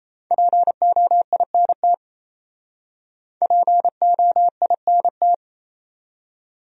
Morse code, heard as a keyed tone, about 37 wpm, 700 Hz. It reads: POINT POINT